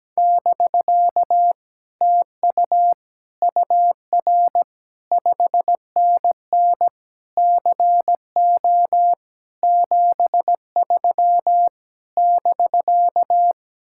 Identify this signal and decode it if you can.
Morse code, heard as a keyed tone, about 17 wpm, 700 Hz. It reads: <BK> TU UR 5NN CO 73 <BK>
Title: Speed-Racing Common QSO Phrases - 17wpm